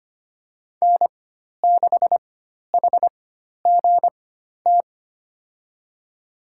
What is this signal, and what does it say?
Morse code, heard as a keyed tone, about 25 wpm, spaced as for 12 wpm, 700 Hz. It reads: N6HGT
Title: USA Call Signs - 4x Character Spacing - 25wpm